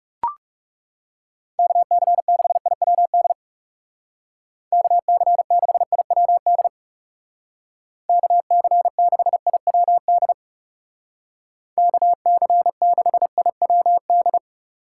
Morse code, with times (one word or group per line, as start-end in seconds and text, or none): 1.6-3.3 KC6IWD
4.7-6.7 KC6IWD
8.1-10.3 KC6IWD
11.8-14.4 KC6IWD